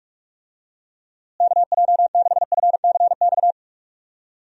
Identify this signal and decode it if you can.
Morse code, heard as a keyed tone, about 42 wpm, 700 Hz. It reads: KJ6FCX
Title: USA Call Signs - 6 Characters - Repeated 3x - 45wpm